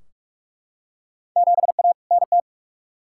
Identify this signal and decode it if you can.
Morse code, heard as a keyed tone, about 45 wpm, 700 Hz. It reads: 7A NT
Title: Field Day Exchanges Encore - 45wpm